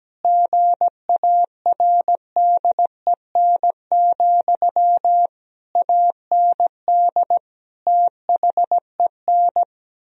Morse code, heard as a keyed tone, about 17 wpm, 700 Hz. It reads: GARDEN, AND THEN
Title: Sentences from Top 1000 Words - Review Part 4 - 17wpm